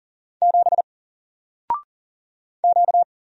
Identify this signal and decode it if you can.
Morse code, heard as a keyed tone, about 40 wpm, 700 Hz. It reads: 7 Q
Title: Single Character (Letter:Number:Punctuation:Prosign) Alt 40wpm